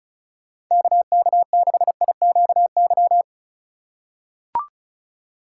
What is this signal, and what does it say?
Morse code, heard as a keyed tone, about 36 wpm, 700 Hz. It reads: KK6IQY
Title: USA Call Signs - 6 Characters - 4x Character Spacing - 35wpm